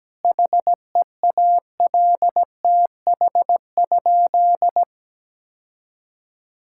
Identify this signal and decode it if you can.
Morse code, heard as a keyed tone, about 17 wpm, 700 Hz. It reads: HEALTH?
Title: Sentences from Top 1000 Words - Review Part 5 - 17wpm